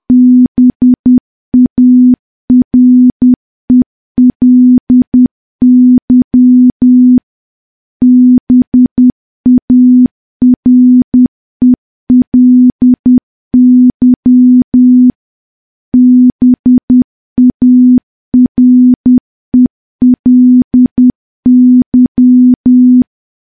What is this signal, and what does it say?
Morse code, heard as a keyed tone, about 10 wpm, 250 Hz. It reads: BARELY BARELY BARELY
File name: morse / morse-training-10WPM